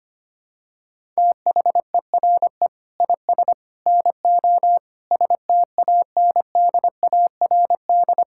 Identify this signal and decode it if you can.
Morse code, heard as a keyed tone, about 25 wpm, 700 Hz. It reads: THERE IS NO STANDARD
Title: Sentences from Top 1000 Words - Review Part 6 - 25wpm